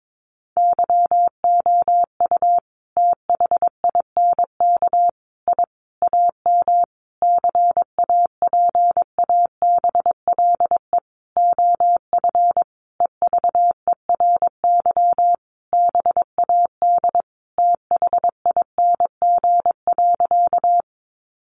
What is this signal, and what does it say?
Morse code, heard as a keyed tone, about 22 wpm, 700 Hz. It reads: YOU THINK I AM CAPABLE OF EVERY BAD THING.
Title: Sentences from Top 1200 Words 22wpm